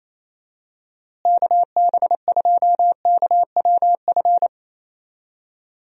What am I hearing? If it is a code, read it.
Morse code, heard as a keyed tone, about 27 wpm, 700 Hz. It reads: KB2KWF
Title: USA Call Signs - 6 Characters - Repeated 2x - 28wpm